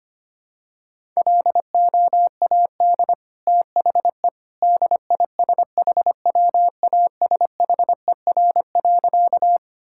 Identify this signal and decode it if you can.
Morse code, heard as a keyed tone, about 25 wpm, 700 Hz. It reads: LOAD THE DISHWASHER.
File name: Easy Sentences with Hints - 25wpm